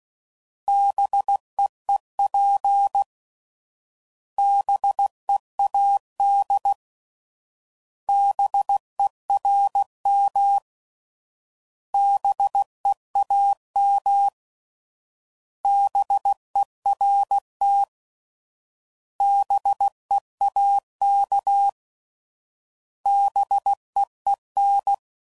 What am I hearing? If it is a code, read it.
Morse code, heard as a keyed tone, about 16 wpm, 800 Hz. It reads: BEEP BEAD BERM BEAM BERT BEAK BEEN